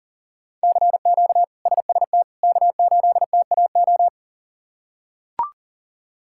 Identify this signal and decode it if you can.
Morse code, heard as a keyed tone, about 40 wpm, 700 Hz. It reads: CQ SST K8TAO
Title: K1USN SST QSO Practice - 40wpm